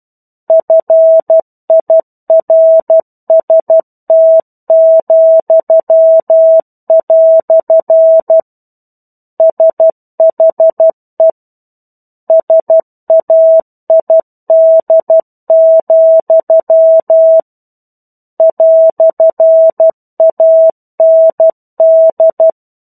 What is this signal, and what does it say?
Morse code, heard as a keyed tone, about 12 wpm, 650 Hz. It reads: FIRST," SHE SAID, "AND
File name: 01-Chapter-01 12 WPM